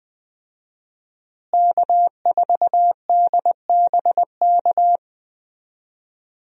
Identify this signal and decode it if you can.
Morse code, heard as a keyed tone, about 20 wpm, 700 Hz. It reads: K4DBK
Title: USA Call Signs - 5 Characters - Repeated 2x - 20wpm